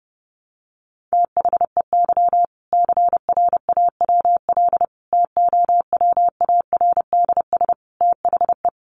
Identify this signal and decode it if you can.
Morse code, heard as a keyed tone, about 30 wpm, 700 Hz. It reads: THEY CRAWL TOWARDS THE
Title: Additional Easy Phrases 30wpm